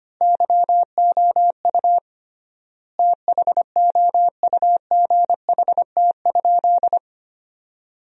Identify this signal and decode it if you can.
Morse code, heard as a keyed tone, about 25 wpm, 700 Hz. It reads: YOU THOUGHT?
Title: 3x Word-Spacing - Sentences from Top 300 Words - 25wpm